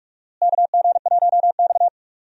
Morse code, heard as a keyed tone, about 46 wpm, 700 Hz. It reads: KG1X
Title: USA Call Signs - 4 Characters - Repeated 2x - 45wpm